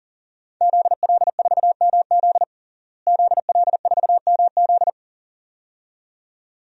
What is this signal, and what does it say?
Morse code, heard as a keyed tone, about 37 wpm, 700 Hz. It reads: ZL4MZ ZL4MZ
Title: DXCC-100 Call Signs - 5 Characters - Repeated 2x - 40wpm